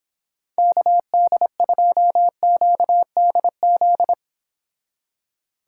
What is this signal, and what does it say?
Morse code, heard as a keyed tone, about 26 wpm, 700 Hz. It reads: KD2QDZ